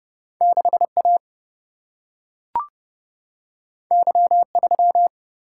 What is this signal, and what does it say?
Morse code, heard as a keyed tone, about 30 wpm, 700 Hz. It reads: 6 A Y 3
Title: Single-Letter Single-Number 30wpm